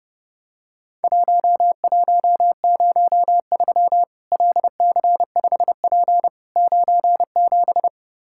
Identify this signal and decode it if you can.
Morse code, heard as a keyed tone, about 29 wpm, 700 Hz. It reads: 1103 LC5P 97
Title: Worked All Europe DX Contest - QTC Exchange - 30wpm